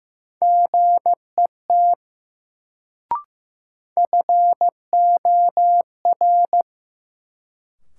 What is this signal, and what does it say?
Morse code, heard as a keyed tone, about 15 wpm, 700 Hz. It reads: GET FOR
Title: Sets of 2 Words - New Word Drill - 15wpm